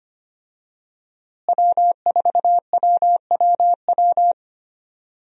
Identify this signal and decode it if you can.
Morse code, heard as a keyed tone, about 25 wpm, 700 Hz. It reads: W4WWW
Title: USA Call Signs - 5 Characters - Repeated 2x - 25wpm